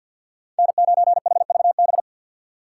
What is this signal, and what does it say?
Morse code, heard as a keyed tone, about 50 wpm, 700 Hz. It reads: N9HVB
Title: USA Call Signs - 3x Character Spacing - 50wpm